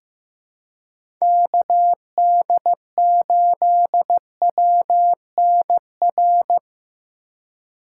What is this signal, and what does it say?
Morse code, heard as a keyed tone, about 16 wpm, 700 Hz. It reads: KD8WNR
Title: USA Call Signs - 6 Characters - 4x Character Spacing - 15wpm